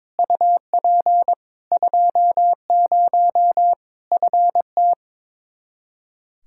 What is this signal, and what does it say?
Morse code, heard as a keyed tone, about 22 wpm, 700 Hz. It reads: UP 20 FT
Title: Practice QSO - Exchange 2 - Info - 22wpm